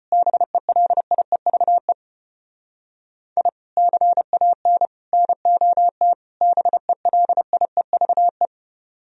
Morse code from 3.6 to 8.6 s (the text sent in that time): CAN NOT BELIEVE